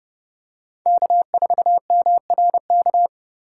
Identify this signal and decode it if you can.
Morse code, heard as a keyed tone, about 32 wpm, 700 Hz. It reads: K4MRK
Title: USA Call Signs - 5 Characters - 3x Character Spacing - 30wpm